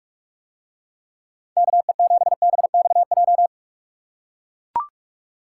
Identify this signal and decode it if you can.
Morse code, heard as a keyed tone, about 44 wpm, 700 Hz. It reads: KE7BXJ